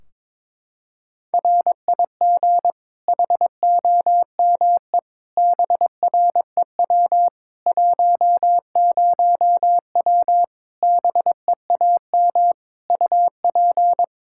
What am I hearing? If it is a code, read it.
Morse code, heard as a keyed tone, about 22 wpm, 700 Hz. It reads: RIG HOME BREW 10W BEAM UP
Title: Practice QSO - Exchange 2 - Info - 22wpm